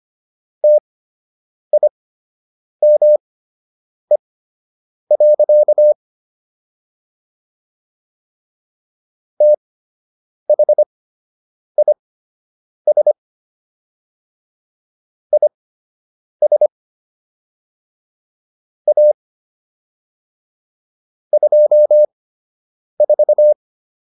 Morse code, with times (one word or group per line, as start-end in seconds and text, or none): 0.6-5.9 TIME.
9.4-13.1 THIS
15.3-16.7 IS
18.9-19.1 A
21.3-23.5 24